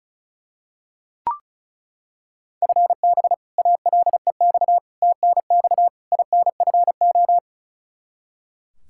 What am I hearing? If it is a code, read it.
Morse code, heard as a keyed tone, about 34 wpm, 700 Hz. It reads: FB ALEX TNX INFO
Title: Practice QSO - Exchange 3 - More Info - 35wpm